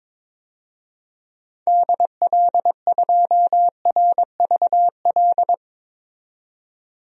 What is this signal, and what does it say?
Morse code, heard as a keyed tone, about 22 wpm, 700 Hz. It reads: DL2RVL
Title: DXCC-100 Call Signs - 6 Characters - 22wpm